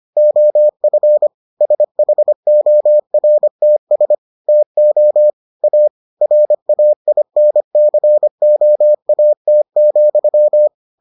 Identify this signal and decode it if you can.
Morse code, heard as a keyed tone, about 25 wpm, 600 Hz. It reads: OF SHORTS TO A RAINCOAT,